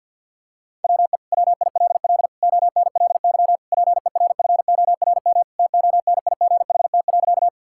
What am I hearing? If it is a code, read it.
Morse code, heard as a keyed tone, about 51 wpm, 690 Hz. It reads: WE WILL ONLY PERFORM TONIGHT.
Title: Sentences from Top 1000 Words - Review Part 5 - 50wpm